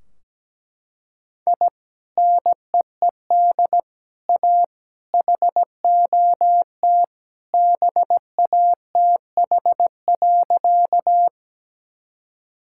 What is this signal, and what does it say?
Morse code, heard as a keyed tone, about 17 wpm, 700 Hz. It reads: I NEED A HOT BATH.
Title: Sentences from Top 1600 Words 17wpm